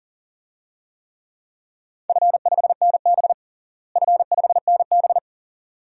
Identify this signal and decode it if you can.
Morse code, heard as a keyed tone, about 40 wpm, 700 Hz. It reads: F5NB F5NB
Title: DXCC-100 Call Signs - 4 Characters - Repeated 2x - 40wpm